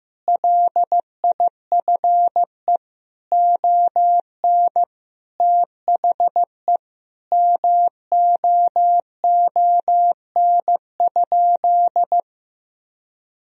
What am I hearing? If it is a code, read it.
Morse code, heard as a keyed tone, about 16 wpm, 700 Hz. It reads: LIFE ON THE MOON?
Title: Sentences from Top 1000 Words - Review Part 3 - 15wpm